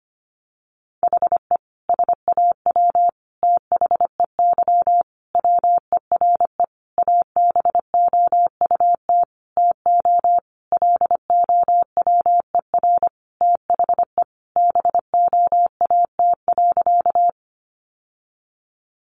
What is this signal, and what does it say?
Morse code, heard as a keyed tone, about 25 wpm, 700 Hz. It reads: HE SAW THEY WERE ABOUT TO LOWER THE BOAT.
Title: Sentences from Top 700 Words 25wpm